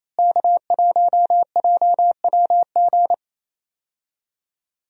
Morse code, heard as a keyed tone, about 28 wpm, 700 Hz. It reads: K1JWG
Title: USA Call Signs - 5 Characters - Repeated 3x - 28wpm